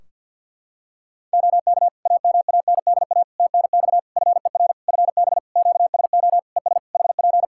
Morse code, heard as a keyed tone, about 51 wpm, 700 Hz. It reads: OK AMANDA TNX FER FB QSO ES HP
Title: Practice QSO - Exchange 4 - Ending - 50wpm